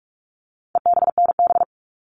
Morse code, heard as a keyed tone, about 44 wpm, 700 Hz. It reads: EBNB
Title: All Cut Numbers - 4-Digit 45wpm